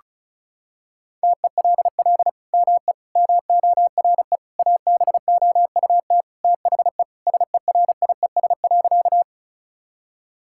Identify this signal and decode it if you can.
Morse code, heard as a keyed tone, about 36 wpm, 700 Hz. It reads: TELL ME MORE ABOUT THE SERIES.
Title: Sentences from Top 1000 Words - Review Part 5 - 35wpm